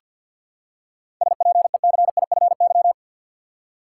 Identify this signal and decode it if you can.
Morse code, heard as a keyed tone, about 50 wpm, 700 Hz. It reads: SPECIFY
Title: Seven Letter Words - 50wpm